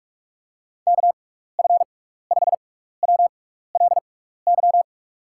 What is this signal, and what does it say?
Morse code, heard as a keyed tone, about 44 wpm, spaced as for 15 wpm, 700 Hz. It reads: KF5WLY
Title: USA Call Signs - 6x Character Spacing - 45wpm